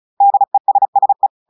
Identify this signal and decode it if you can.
Morse code, heard as a keyed tone, about 34 wpm, 800 Hz. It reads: DESSE
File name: audios / portugais / portugais texte en clair 2 vitesse 35 mots minutes